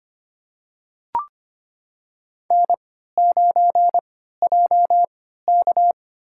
Morse code, heard as a keyed tone, about 26 wpm, spaced as for 14 wpm, 700 Hz. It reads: N9JK
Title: USA Call Signs - 4 Characters - 3x Character Spacing - 25wpm